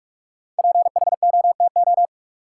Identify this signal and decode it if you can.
Morse code, heard as a keyed tone, about 43 wpm, 700 Hz. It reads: PHOTO
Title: Sentences from Top 1200 Words - New Word Drill - 45wpm